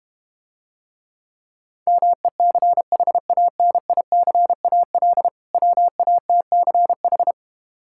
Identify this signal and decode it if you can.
Morse code, heard as a keyed tone, about 32 wpm, 700 Hz. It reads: MECHANICAL WATCH